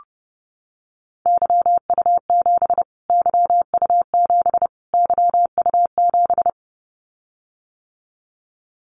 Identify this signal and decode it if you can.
Morse code, heard as a keyed tone, about 28 wpm, 700 Hz. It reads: YU7 YU7 YU7
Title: Callsign Prefixes for Top 25 DXCC - Repeated 3x - 30wpm